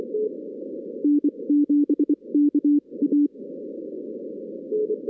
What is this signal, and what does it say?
Morse code, heard as a keyed tone, about 24 wpm, spaced as for 21 wpm, 290 Hz. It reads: N7KA
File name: 2016 arrl dx cw / 2016-ARRL-DX-CW-K5TR- - 0-39